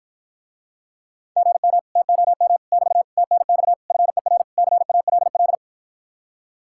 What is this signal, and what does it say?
Morse code, heard as a keyed tone, about 54 wpm, 700 Hz. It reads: GM TOM <BT> TNX FER CALL